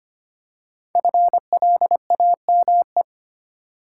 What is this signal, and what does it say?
Morse code, heard as a keyed tone, about 25 wpm, 700 Hz. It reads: FLAME
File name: 5 Letter Words - First 3 Letters Given - 25wpm